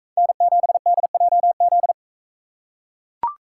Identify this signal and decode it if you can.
Morse code, heard as a keyed tone, about 42 wpm, 700 Hz. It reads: N7DJZ